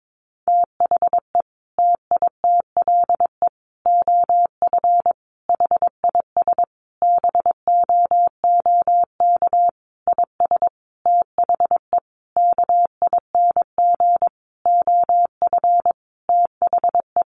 Morse code, heard as a keyed tone, about 22 wpm, 700 Hz. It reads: THE TITLE OF HIS BOOK IS THE KING OF THE
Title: Sentences from Top 1200 Words 22wpm